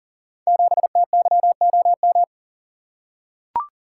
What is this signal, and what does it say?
Morse code, heard as a keyed tone, about 39 wpm, 700 Hz. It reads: 7TYOM
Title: Code Groups - 5 Letter:Numbers - 40wpm